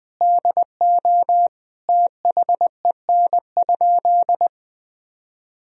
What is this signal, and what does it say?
Morse code, heard as a keyed tone, about 20 wpm, 700 Hz. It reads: DO THEN?
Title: Sentences from Top 100 Words and Abbreviations 20wpm